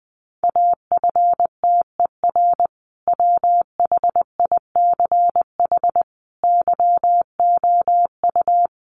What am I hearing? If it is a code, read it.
Morse code, heard as a keyed tone, about 20 wpm, 700 Hz. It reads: AFTER WHICH YOU
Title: Sentences from Top 1400 Words 20wpm